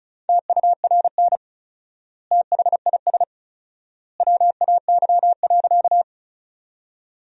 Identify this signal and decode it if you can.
Morse code, heard as a keyed tone, about 36 wpm, 700 Hz. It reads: TURN THIS WAY.
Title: 4x Word-Spacing - Sentences from Top 200 Words - 35wpm